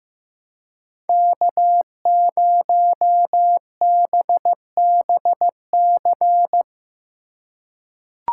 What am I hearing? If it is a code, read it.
Morse code, heard as a keyed tone, about 14 wpm, 700 Hz. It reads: K0BBC
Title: USA Call Signs - 5 Characters - 3x Character Spacing - 15wpm